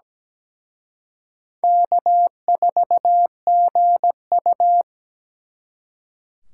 Morse code, heard as a keyed tone, about 20 wpm, 700 Hz. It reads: K4GU
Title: Speed Racing - USA Call Signs 17wpm